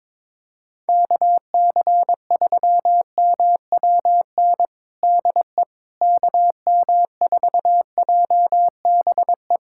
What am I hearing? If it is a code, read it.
Morse code, heard as a keyed tone, about 22 wpm, 700 Hz. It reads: KC3MWN DE KM4JBE
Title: USA Callsign DE Callsign - 6 Characters - 22wpm